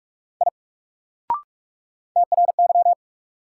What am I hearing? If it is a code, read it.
Morse code, heard as a keyed tone, about 42 wpm, 700 Hz. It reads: I TRY
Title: Sentences from Top 200 Words - New Word Drill - 45wpm